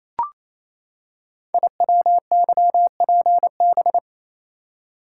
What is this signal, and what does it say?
Morse code, heard as a keyed tone, about 27 wpm, 700 Hz. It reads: IWYPB
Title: Code Groups - 5 Letters - 28wpm